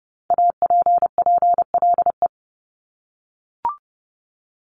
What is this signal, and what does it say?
Morse code, heard as a keyed tone, about 30 wpm, 700 Hz. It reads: APPLE
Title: Multi-letter Phonemes Repeated 3x 30wpm